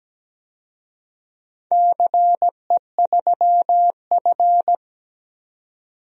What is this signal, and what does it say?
Morse code, heard as a keyed tone, about 17 wpm, 700 Hz. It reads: CE3F
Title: DXCC-100 Call Signs - 4 Characters - 17wpm